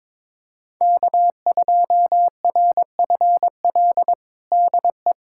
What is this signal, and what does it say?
Morse code, heard as a keyed tone, about 21 wpm, 700 Hz. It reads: K2RFL DE
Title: USA Callsign DE Callsign - 5 Characters - 22wpm